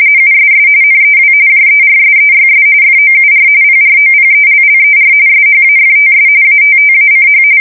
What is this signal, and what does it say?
RTTY, about 45.45 baud. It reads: USS NEW JERSEYS FOUR PROPELLER SHAFTS ATTACH
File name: fsk170 / battleshipnj